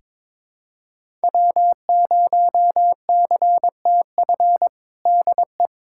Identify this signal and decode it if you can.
Morse code, heard as a keyed tone, about 22 wpm, 700 Hz. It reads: W0CTF DE
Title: USA Callsign DE Callsign - 5 Characters - 22wpm